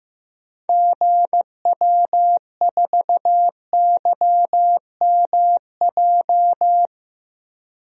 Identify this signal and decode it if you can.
Morse code, heard as a keyed tone, about 15 wpm, 700 Hz. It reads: GW4YMJ
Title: DXCC-100 Call Signs - 6 Characters - 15wpm